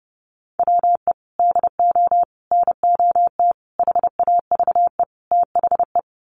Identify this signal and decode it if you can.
Morse code, heard as a keyed tone, about 29 wpm, 700 Hz. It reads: WE DO NOT HAVE THE
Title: Sets of 5 Words 30wpm